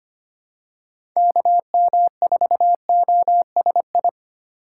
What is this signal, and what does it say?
Morse code, heard as a keyed tone, about 25 wpm, 700 Hz. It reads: KM4OSI
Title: USA Call Signs - 6 Characters - 4x Character Spacing - 25wpm